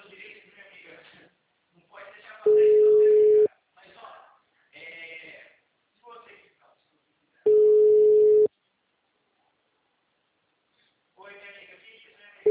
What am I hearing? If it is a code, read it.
Ringback tone (Europe, 425 Hz), with speech.
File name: out- -2019- - - .30192